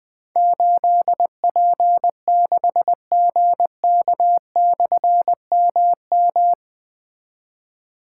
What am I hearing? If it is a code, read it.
Morse code, heard as a keyed tone, about 20 wpm, 700 Hz. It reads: 8P6GK/MM
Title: Difficult Call Signs 20wpm